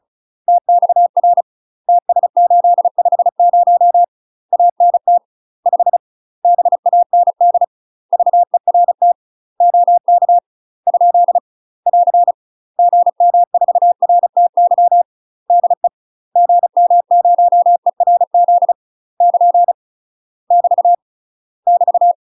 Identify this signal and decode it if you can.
Morse code, heard as a keyed tone, about 34 wpm, 700 Hz. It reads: TXR TS850 ANT 5 BAND VERT OK ? <AR> GM4RTY DE GM0ERZ <KN> <BT> <BT>
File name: UK MASTER QSOs TRAINING / 35WPM / UK MASTER QSOs 0003